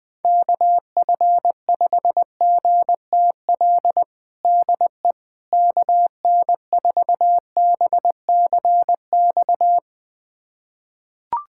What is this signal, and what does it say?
Morse code, heard as a keyed tone, about 20 wpm, 700 Hz. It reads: KF5GTL DE KN4BCX
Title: USA Callsign DE Callsign 20wpm